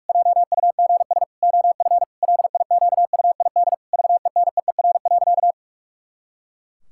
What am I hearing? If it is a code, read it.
Morse code, heard as a keyed tone, about 46 wpm, 700 Hz. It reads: JUGS OF LIQUID VENEER.
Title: Pangram Challenge v2 - 45wpm